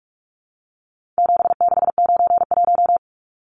Morse code, heard as a keyed tone, about 44 wpm, 700 Hz. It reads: 7691
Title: Numbers - 4-Digit 45wpm